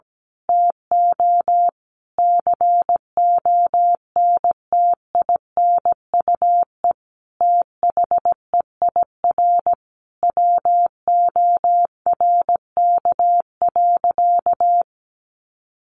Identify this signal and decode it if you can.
Morse code, heard as a keyed tone, about 17 wpm, 700 Hz. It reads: TO CONTINUE THEIR WORK.
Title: Sentences from Top 1100 Words 17wpm